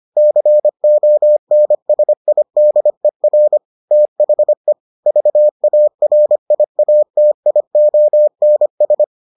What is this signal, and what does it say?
Morse code, heard as a keyed tone, about 25 wpm, 600 Hz. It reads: CONSIDER THE VARIATIONS